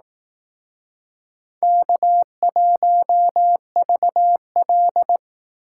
Morse code, heard as a keyed tone, about 20 wpm, 700 Hz. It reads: K1VL